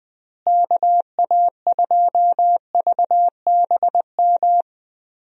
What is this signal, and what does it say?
Morse code, heard as a keyed tone, about 20 wpm, 700 Hz. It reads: KA2VBM
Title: USA Call Signs - 6 Characters - Repeated 2x - 20wpm